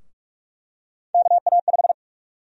Morse code, heard as a keyed tone, about 42 wpm, 700 Hz. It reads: KA5
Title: USA Call Sign Prefixes - Repeated 2x - 45wpm